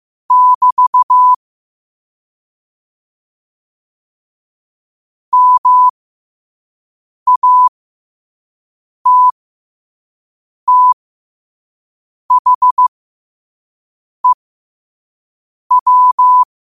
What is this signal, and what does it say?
Morse code, heard as a keyed tone, about 15 wpm, spaced as for 5 wpm, 1000 Hz. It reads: <BT> MATTHEW
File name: qotdfiles / 5wpm